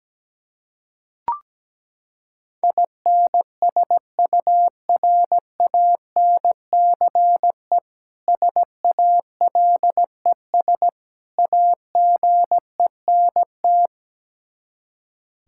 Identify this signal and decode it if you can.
Morse code, heard as a keyed tone, about 17 wpm, 700 Hz. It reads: INSURANCE SALES AGENT
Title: Practice QSO - Exchange 3 - More Info - 17wpm